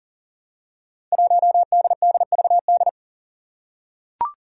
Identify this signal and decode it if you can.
Morse code, heard as a keyed tone, about 40 wpm, 700 Hz. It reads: 1DDVD
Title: Five Challenging Letter-Number Combinations 40wpm